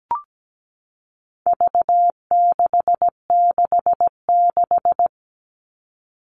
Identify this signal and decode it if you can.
Morse code, heard as a keyed tone, about 17 wpm, 700 Hz. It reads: V666
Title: All Cut Numbers - 4-Digit 17wpm